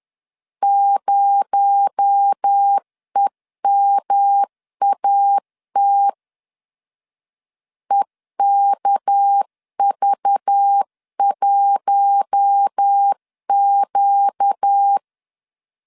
Morse code, heard as a keyed tone, about 10 wpm, 800 Hz. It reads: 0EMAT EKV1Q
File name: LEZIONI / 7 GF9Z / 7 GF 10wpm 2514